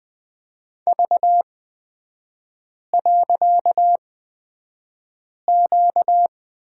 Morse code, heard as a keyed tone, about 20 wpm, spaced as for 5 wpm, 690 Hz. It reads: V.Q
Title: 255 Lesson 41e - Period - Three Characters 20wpm